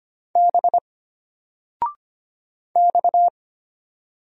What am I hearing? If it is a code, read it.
Morse code, heard as a keyed tone, about 25 wpm, 700 Hz. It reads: B X
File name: Single-Letter-Number-25wpm